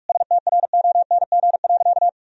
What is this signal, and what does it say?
Morse code, heard as a keyed tone, about 46 wpm, 700 Hz. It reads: STRONG.
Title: Sentences from Top 1000 Words - Review Part 4 - 45wpm